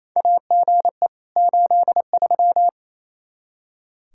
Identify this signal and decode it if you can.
Morse code, heard as a keyed tone, about 28 wpm, 700 Hz. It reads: AGE 83
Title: K1USN SST QSO Practice 28wpm